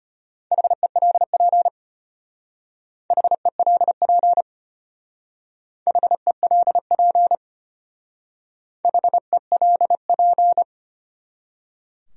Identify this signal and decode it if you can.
Morse code, heard as a keyed tone, about 32 wpm, 700 Hz. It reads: HELP HELP HELP HELP